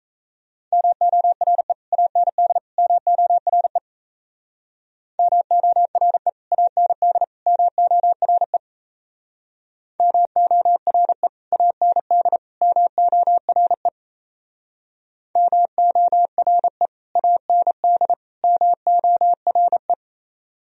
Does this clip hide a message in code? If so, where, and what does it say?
Yes: Morse code from 0.7 to 19.9 s, heard as a keyed tone, about 37 wpm, 700 Hz. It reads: MORE AND MORE MORE AND MORE MORE AND MORE MORE AND MORE